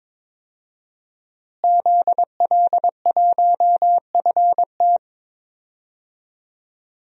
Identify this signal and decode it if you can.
Morse code, heard as a keyed tone, about 22 wpm, 700 Hz. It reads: ZL1FT
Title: DXCC-100 Call Signs - 5 Characters - Repeated 2x - 22wpm